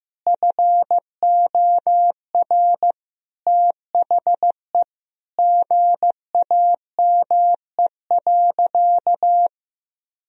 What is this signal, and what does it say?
Morse code, heard as a keyed tone, about 15 wpm, 700 Hz. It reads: FOR THE GAME.
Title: Sentences from Top 1000 Words - Review Part 4 - 15wpm